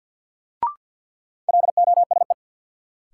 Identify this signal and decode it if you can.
Morse code, heard as a keyed tone, about 49 wpm, 700 Hz. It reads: LOSE
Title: Sentences from Top 1100 Words - New Word Drill - 50wpm